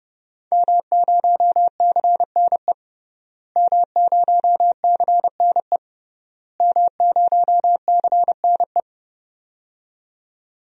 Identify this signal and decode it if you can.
Morse code, heard as a keyed tone, about 30 wpm, 700 Hz. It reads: M0CNE M0CNE M0CNE
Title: DXCC-100 Call Signs - 5 Characters - Repeated 3x - 30wpm